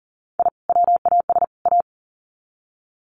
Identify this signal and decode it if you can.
Morse code, heard as a keyed tone, about 38 wpm, 700 Hz. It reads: I WAS A
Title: Sets of 3 Words 40wpm